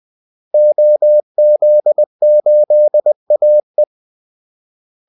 Morse code, heard as a keyed tone, about 20 wpm, 600 Hz. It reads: OZ8AE
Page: Kaldesignaler 20 wpm